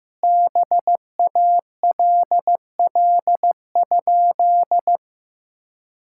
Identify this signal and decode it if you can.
Morse code, heard as a keyed tone, about 15 wpm, 700 Hz. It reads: BALL?
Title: Sentences from Top 1000 Words - Review Part 6 - 15wpm